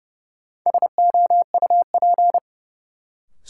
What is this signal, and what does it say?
Morse code, heard as a keyed tone, about 30 wpm, 700 Hz. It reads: SOUP
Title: Sentences from Top 1600 Words - New Word Drill Alt - 30wpm